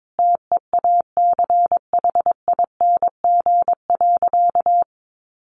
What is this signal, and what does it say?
Morse code, heard as a keyed tone, about 22 wpm, 700 Hz. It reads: TEACHING.
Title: Sentences from Top 1100 Words 22wpm